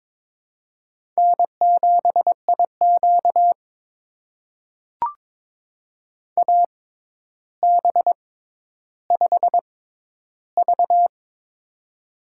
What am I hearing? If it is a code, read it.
Morse code, heard as a keyed tone, about 22 wpm, spaced as for 11 wpm, 700 Hz. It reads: N7IQ AB5V
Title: USA Call Signs - 4 Characters - 6x Character Spacing - 22wpm